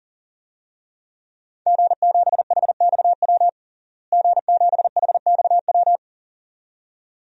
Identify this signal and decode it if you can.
Morse code, heard as a keyed tone, about 40 wpm, 700 Hz. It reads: G7HXW G7HXW
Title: DXCC-100 Call Signs - 5 Characters - Repeated 2x - 40wpm